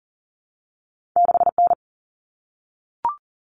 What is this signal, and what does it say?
Morse code, heard as a keyed tone, about 41 wpm, 700 Hz. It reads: 6N
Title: Common Cut Numbers - 2-Digit 40wpm